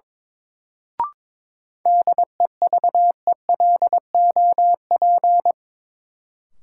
Morse code, heard as a keyed tone, about 22 wpm, 700 Hz. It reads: DEVELOP
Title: Sentences from Top 1000 Words - Review Part 2 - New Word Drill - 22wpm